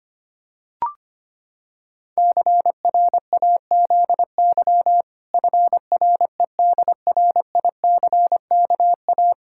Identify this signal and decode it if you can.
Morse code, heard as a keyed tone, about 25 wpm, 700 Hz. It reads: CRAZY FREDRICKA
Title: Pangram Challenge v2 - 25wpm